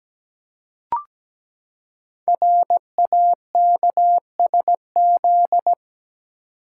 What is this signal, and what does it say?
Morse code, heard as a keyed tone, about 17 wpm, 700 Hz. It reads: RAKSZ
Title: Code Groups - 5 Letter:Numbers - 17wpm